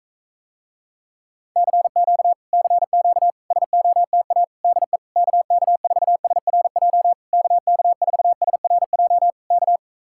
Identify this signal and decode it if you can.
Morse code, heard as a keyed tone, about 45 wpm, 700 Hz. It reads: CQ CQ SOTA DE KK4SRJ KK4SRJ K